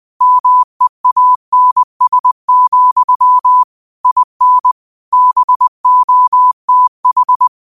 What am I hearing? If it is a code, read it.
Morse code, heard as a keyed tone, about 20 wpm, 1000 Hz. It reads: MEANS, IN BOTH
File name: qotdfiles / 20wpm